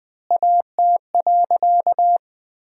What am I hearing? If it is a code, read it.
Morse code, heard as a keyed tone, about 21 wpm, 700 Hz. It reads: AT.
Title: 4x Word-Spacing - Sentences from Top 200 Words - 20wpm